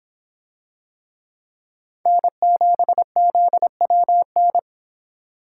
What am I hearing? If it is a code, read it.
Morse code, heard as a keyed tone, about 26 wpm, 700 Hz. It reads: N7ZWN